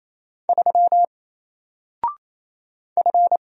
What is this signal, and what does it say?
Morse code, heard as a keyed tone, about 27 wpm, 700 Hz. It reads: 3 F
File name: Single-Letter-Number-28wpm